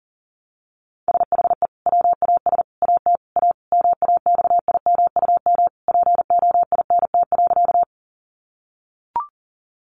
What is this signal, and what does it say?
Morse code, heard as a keyed tone, about 37 wpm, 700 Hz. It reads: SHE WAS AT A MAXIMUM POINT.
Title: Sentences from Top 1300 Words 40wpm